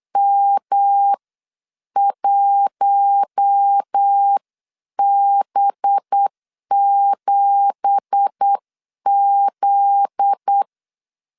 Morse code, heard as a keyed tone, about 8 wpm, spaced as for 7 wpm, 800 Hz. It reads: M1B7Z
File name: CW / DET / 1 9 1500C S B 8wpm 5624